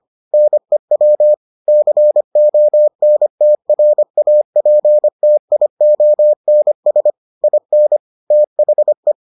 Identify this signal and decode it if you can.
Morse code, heard as a keyed tone, about 25 wpm, 600 Hz. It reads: NEW CONTRAPTIONS IN THE